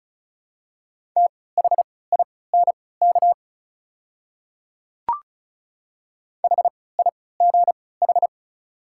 Morse code, heard as a keyed tone, about 34 wpm, spaced as for 20 wpm, 700 Hz. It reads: THINK HIGH
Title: 3x Character Spacing - Top 300 Words - 35wpm